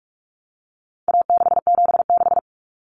Morse code, heard as a keyed tone, about 44 wpm, 700 Hz. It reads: A676
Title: Common Cut Numbers - 4-Digit 45wpm